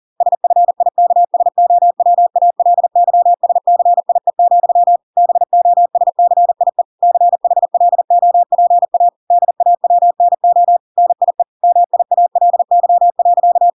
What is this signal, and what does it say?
Morse code, heard as a keyed tone, about 40 wpm, 700 Hz. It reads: SFIKSOWALYSCIE, BOSCIE CHLOPA DAWNO NIE MIALY.